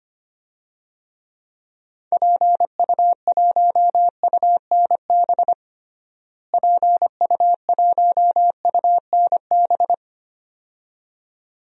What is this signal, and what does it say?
Morse code, heard as a keyed tone, about 25 wpm, 700 Hz. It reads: PU1UNB PU1UNB
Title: DXCC-100 Call Signs - 6 Characters - Repeated 2x - 25wpm